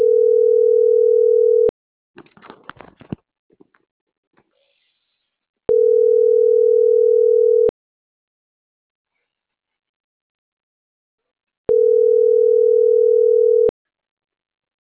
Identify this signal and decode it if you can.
Ringback tone (North America), 440 + 480 Hz.